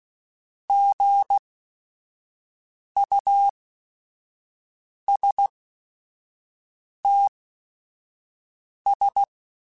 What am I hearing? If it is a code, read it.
Morse code, heard as a keyed tone, about 16 wpm, spaced as for 5 wpm, 800 Hz. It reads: GUSTS
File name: Weather Sets / Weather Sets-mp3s / Weather 5 5 @5